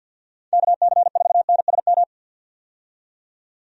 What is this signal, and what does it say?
Morse code, heard as a keyed tone, about 50 wpm, 700 Hz. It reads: KC4NSM
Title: USA Call Signs - 6 Characters - 50wpm